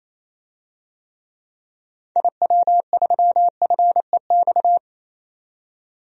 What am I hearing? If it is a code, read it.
Morse code, heard as a keyed tone, about 28 wpm, 700 Hz. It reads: IW3FEX
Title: DXCC-100 Call Signs - 6 Characters - 28wpm